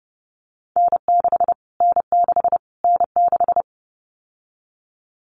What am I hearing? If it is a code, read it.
Morse code, heard as a keyed tone, about 30 wpm, 700 Hz. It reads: N6 N6 N6
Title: Callsign Prefixes for Top 25 DXCC - Repeated 3x - 30wpm